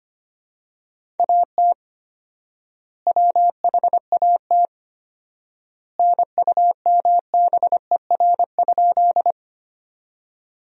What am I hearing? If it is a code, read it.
Morse code, heard as a keyed tone, about 25 wpm, 700 Hz. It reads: AT WHAT NUMBER?
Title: 4x Word-Spacing - Sentences from Top 100 Words - 25wpm